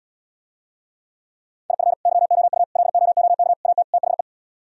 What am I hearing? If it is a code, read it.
Morse code, heard as a keyed tone, about 77 wpm, 700 Hz. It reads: IF YOU LOOK AT THE